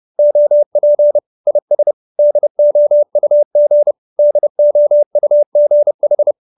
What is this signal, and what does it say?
Morse code, heard as a keyed tone, about 30 wpm, 590 Hz. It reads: OP IS DOUG DOUGH